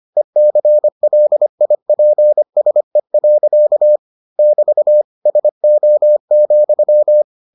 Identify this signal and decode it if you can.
Morse code, heard as a keyed tone, about 25 wpm, 600 Hz. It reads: ECLIPSE. <BT> SO,